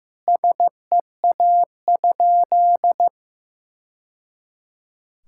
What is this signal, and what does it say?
Morse code, heard as a keyed tone, about 15 wpm, 700 Hz. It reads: SEA?
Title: Sentences from Top 1000 Words - Review Part 6 - 15wpm